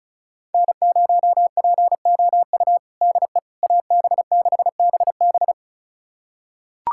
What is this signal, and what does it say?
Morse code, heard as a keyed tone, about 36 wpm, 700 Hz. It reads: N0POU DE AB6BB
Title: USA Callsign DE Callsign - 5 Characters - 35wpm